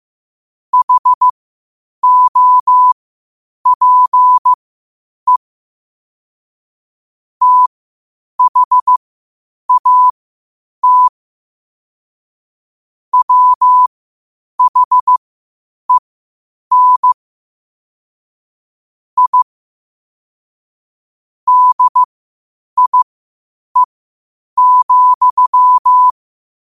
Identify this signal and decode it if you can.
Morse code, heard as a keyed tone, about 15 wpm, spaced as for 8 wpm, 1000 Hz. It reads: HOPE THAT WHEN I DIE,
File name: qotdfiles / 8wpm